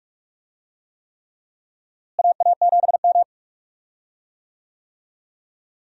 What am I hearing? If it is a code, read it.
Morse code, heard as a keyed tone, about 42 wpm, 700 Hz. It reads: AA7M
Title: USA Call Signs - 4 Characters - Repeated 3x - 45wpm